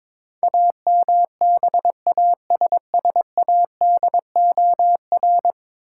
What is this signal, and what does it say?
Morse code, heard as a keyed tone, about 23 wpm, 700 Hz. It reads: AMBASSADOR
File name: practice / 22wpm / 10 Letter Words - First 3 Letters Given - 22wpm